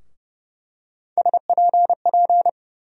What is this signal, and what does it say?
Morse code, heard as a keyed tone, about 31 wpm, 700 Hz. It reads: SPP
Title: USA Call Sign Postfixes - 3 Letters 30wpm